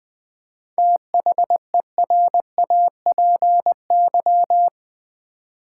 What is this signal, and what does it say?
Morse code, heard as a keyed tone, about 20 wpm, 700 Hz. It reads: THERAPY
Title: 7 Letter Words - First 3 Letters Given - 20wpm